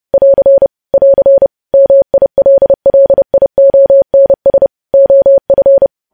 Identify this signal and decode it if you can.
Morse code, heard as a keyed tone, about 30 wpm, 550 Hz. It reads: <AR> <AR> MILLIONS OF
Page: news headlines in morse code at 30 wpm